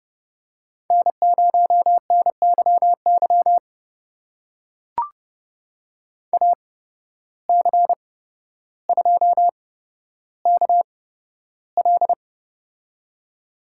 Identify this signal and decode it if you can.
Morse code, heard as a keyed tone, about 30 wpm, spaced as for 13 wpm, 700 Hz. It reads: N0NYY AC2KL
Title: USA Call Signs - 5 Characters - 8x Character Spacing - 30wpm